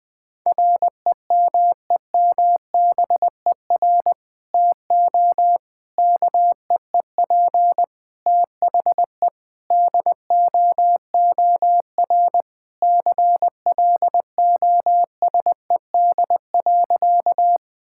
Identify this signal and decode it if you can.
Morse code, heard as a keyed tone, about 21 wpm, 700 Hz. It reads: REMEMBER TO KEEP THE DOOR CLOSED.
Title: Sentences with 3-Word Phrases - 20wpm